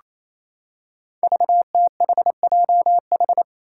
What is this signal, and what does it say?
Morse code, heard as a keyed tone, about 27 wpm, 700 Hz. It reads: VTHJH
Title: Five Challenging Letter-Number Combinations 28wpm